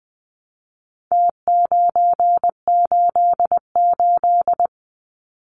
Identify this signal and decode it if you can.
Morse code, heard as a keyed tone, about 20 wpm, 700 Hz. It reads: T988
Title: Numbers - Contest Style - 4-Digit 20wpm